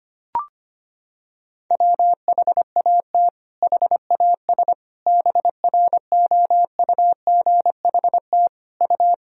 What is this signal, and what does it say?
Morse code, heard as a keyed tone, about 25 wpm, 700 Hz. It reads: WHAT HAS BROUGHT U
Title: Sentences from Top 500 Words and Abbreviations 25wpm